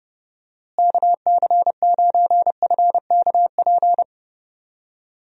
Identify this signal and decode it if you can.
Morse code, heard as a keyed tone, about 29 wpm, 700 Hz. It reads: KC9FKP
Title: USA Call Signs - 6 Characters - Repeated 2x - 30wpm